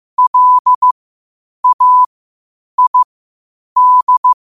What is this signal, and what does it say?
Morse code, heard as a keyed tone, about 15 wpm, spaced as for 8 wpm, 1000 Hz. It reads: LAID
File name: qotdfiles / 8wpm